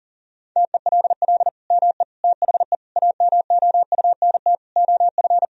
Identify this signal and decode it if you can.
Morse code, heard as a keyed tone, about 40 wpm, 700 Hz. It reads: TELL ME THE AMOUNT OF
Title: Sentences from Top 1000 Words - Review Part 4 - 40wpm